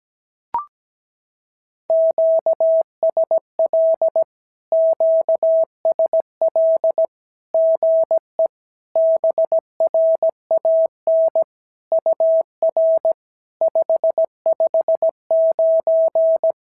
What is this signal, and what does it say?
Morse code, heard as a keyed tone, about 17 wpm, 650 Hz. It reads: QSL QSL GE BRAN UR 559
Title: SOTA QSO Practice 17wpm